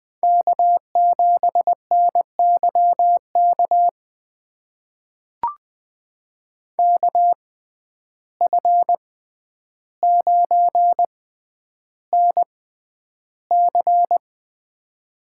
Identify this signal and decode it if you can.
Morse code, heard as a keyed tone, about 20 wpm, spaced as for 10 wpm, 700 Hz. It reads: K7NYK KF9NC
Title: USA Call Signs - 5 Characters - 6x Character Spacing - 20wpm